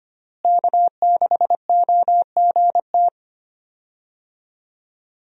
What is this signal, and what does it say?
Morse code, heard as a keyed tone, about 25 wpm, 700 Hz. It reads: K6OGT
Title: USA Call Signs - 5 Characters - Repeated 3x - 25wpm